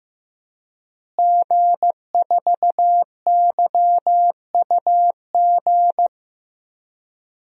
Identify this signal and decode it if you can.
Morse code, heard as a keyed tone, about 15 wpm, 700 Hz. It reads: G4YUG
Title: DXCC-100 Call Signs - 5 Characters - Repeated 3x - 15wpm